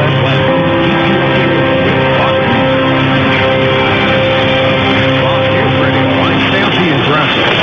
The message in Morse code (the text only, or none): none